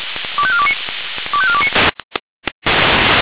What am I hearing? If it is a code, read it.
Five-tone selective call (ZVEI), heard as quick runs of five tones, tones of about 70 ms. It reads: 25429 25429